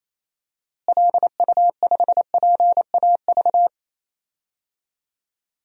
Morse code, heard as a keyed tone, about 28 wpm, 700 Hz. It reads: LU5PAV
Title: DXCC-100 Call Signs - 6 Characters - Repeated 2x - 28wpm